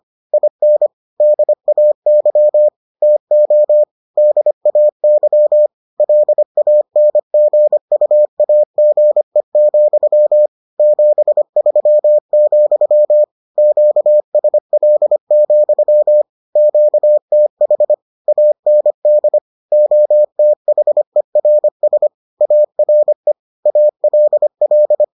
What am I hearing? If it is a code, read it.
Morse code, heard as a keyed tone, about 25 wpm, 600 Hz. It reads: IN DAY TO DAY LANGUAGE, 73, QSL, QTH AND OTHERS ARE ALL